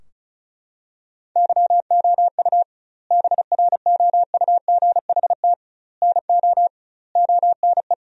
Morse code, heard as a keyed tone, about 36 wpm, 700 Hz. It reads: YOU BROUGHT NO ONE
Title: 2x Word-Spacing - Sentences from Top 500 Words - 35wpm